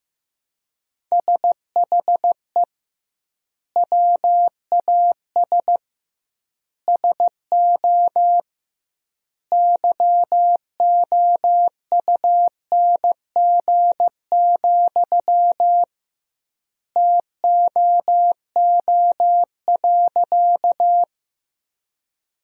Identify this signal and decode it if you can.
Morse code, heard as a keyed tone, about 16 wpm, 700 Hz. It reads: SHE WAS SO YOUNG, TOO.
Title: 2x Word-Spacing - Sentences from Top 400 Words - 15wpm